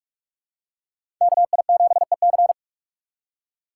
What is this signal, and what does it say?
Morse code, heard as a keyed tone, about 46 wpm, 700 Hz. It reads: KI7EC
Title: USA Call Signs - 2x Character Spacing - 45wpm